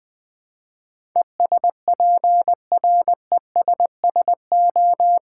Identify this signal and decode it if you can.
Morse code, heard as a keyed tone, about 21 wpm, 690 Hz. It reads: ESPRESSO
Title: Easy Words with Hints - 20wpm